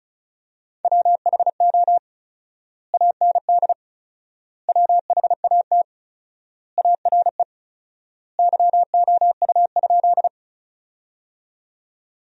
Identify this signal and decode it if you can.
Morse code, heard as a keyed tone, about 36 wpm, 700 Hz. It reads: WHO AND WHAT ARE YOU?
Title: 4x Word-Spacing - Sentences from Top 100 Words - 35wpm